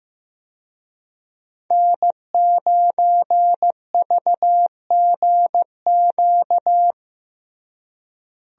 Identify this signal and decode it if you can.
Morse code, heard as a keyed tone, about 15 wpm, 700 Hz. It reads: N9VGQ
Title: USA Call Signs - 5 Characters - Repeated 2x - 15wpm